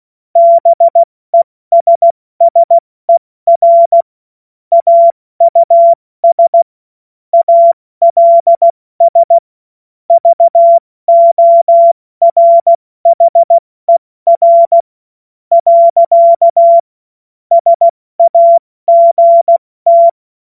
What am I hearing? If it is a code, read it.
Morse code, heard as a keyed tone, about 16 wpm, 680 Hz. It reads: BESSER AUS ALS VORHER . SAGT
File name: Morsekurs / Witze / Witz 02 April 2024 70 F80